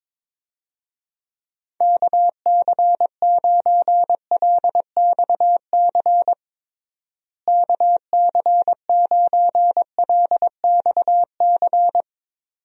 Morse code, heard as a keyed tone, about 22 wpm, 700 Hz. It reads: KC9LXC KC9LXC
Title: USA Call Signs - 6 Characters - Repeated 2x - 22wpm